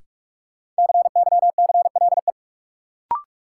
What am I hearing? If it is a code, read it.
Morse code, heard as a keyed tone, about 42 wpm, 700 Hz. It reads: CYCLE
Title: Sentences from Top 900 Words - New Word Drill - 45wpm